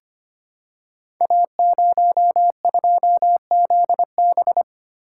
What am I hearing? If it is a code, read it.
Morse code, heard as a keyed tone, about 25 wpm, 700 Hz. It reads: A02ZB
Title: Code Groups - 5 Letter:Numbers - 25wpm